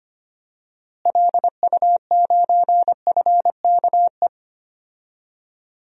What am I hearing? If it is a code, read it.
Morse code, heard as a keyed tone, about 25 wpm, 700 Hz. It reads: LU9FKE
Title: DXCC-100 Call Signs - 6 Characters - Repeated 3x - 25wpm